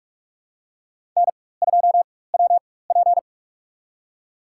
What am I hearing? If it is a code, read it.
Morse code, heard as a keyed tone, about 46 wpm, spaced as for 21 wpm, 700 Hz. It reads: N2WP
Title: USA Call Signs - 4 Characters - 4x Character Spacing - 45wpm